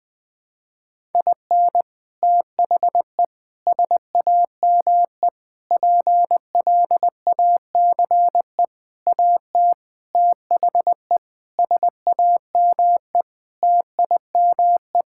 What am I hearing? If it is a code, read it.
Morse code, heard as a keyed tone, about 20 wpm, 700 Hz. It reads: IN THE SAME PLACE AT THE SAME TIME
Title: Sets of 8 Words - 20wpm